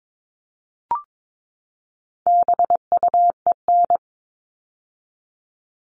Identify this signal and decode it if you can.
Morse code, heard as a keyed tone, about 22 wpm, 700 Hz. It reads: BUEN
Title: All Cut Numbers - 4-Digit 22wpm